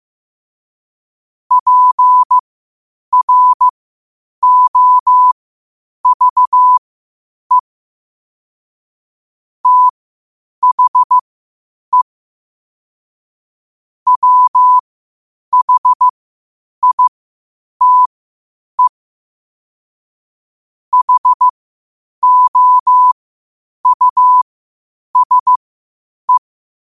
Morse code, heard as a keyed tone, about 15 wpm, spaced as for 8 wpm, 1000 Hz. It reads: PROVE THE WHITE HOUSE
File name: qotdfiles / 8wpm